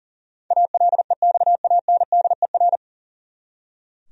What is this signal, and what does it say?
Morse code, heard as a keyed tone, about 40 wpm, 700 Hz. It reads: ALEXANDER
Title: USA Common Names - 40wpm